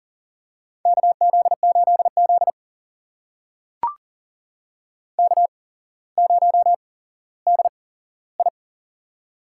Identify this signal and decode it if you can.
Morse code, heard as a keyed tone, about 40 wpm, spaced as for 17 wpm, 700 Hz. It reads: KZ8Z K0DI
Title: USA Call Signs - 4 Characters - 8x Character Spacing - 40wpm